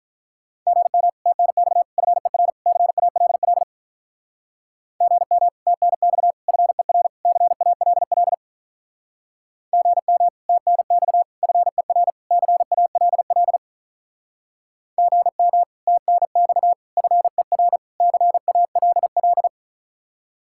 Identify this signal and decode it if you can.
Morse code, heard as a keyed tone, about 45 wpm, 700 Hz. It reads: GM TNX FER CALL GM TNX FER CALL GM TNX FER CALL GM TNX FER CALL